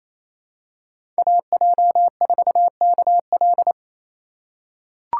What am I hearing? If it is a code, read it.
Morse code, heard as a keyed tone, about 27 wpm, 700 Hz. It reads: AJ4KL
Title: USA Call Signs - 5 Characters - 4x Character Spacing - 28wpm